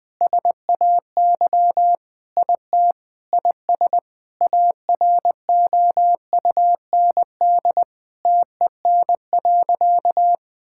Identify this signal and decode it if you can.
Morse code, heard as a keyed tone, about 20 wpm, 700 Hz. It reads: SAY IT IS AROUND TEN.
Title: Sentences with 6-Word Phrases - 20wpm